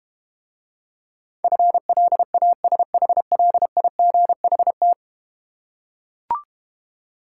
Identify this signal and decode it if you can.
Morse code, heard as a keyed tone, about 32 wpm, 700 Hz. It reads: FLASHLIGHT